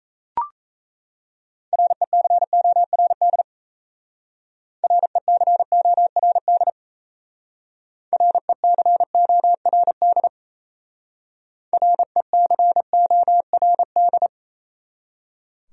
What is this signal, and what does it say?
Morse code, heard as a keyed tone, about 33 wpm, 700 Hz. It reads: RECORD RECORD RECORD RECORD
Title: Speed-Racing Top 500 Words 28wpm